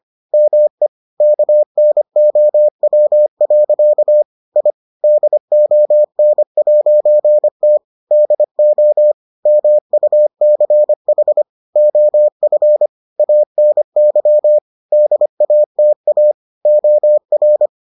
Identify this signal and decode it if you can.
Morse code, heard as a keyed tone, about 25 wpm, 600 Hz. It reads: ME KNOW. I DON'T DO MUCH OF ANY DATA OR